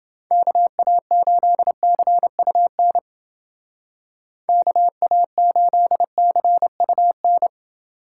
Morse code, heard as a keyed tone, about 29 wpm, 700 Hz. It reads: KA8CUN KA8CUN